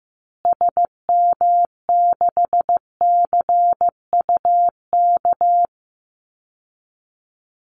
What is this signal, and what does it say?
Morse code, heard as a keyed tone, about 15 wpm, 700 Hz. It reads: SM6CUK
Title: Call Signs Repeated 2x - 15wpm